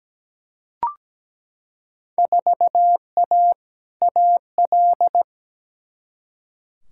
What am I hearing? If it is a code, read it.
Morse code, heard as a keyed tone, about 17 wpm, 700 Hz. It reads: 4A AL
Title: Field Day Exchanges - 17wpm